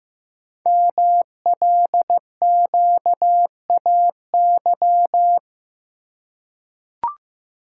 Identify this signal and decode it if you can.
Morse code, heard as a keyed tone, about 15 wpm, 700 Hz. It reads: MLQAY
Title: Code Groups - 5 Letters - 15wpm